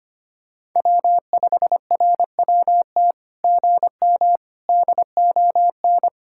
Morse code, heard as a keyed tone, about 25 wpm, 690 Hz. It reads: W5RWT GM DON
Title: SOTA QSO Practice 25wpm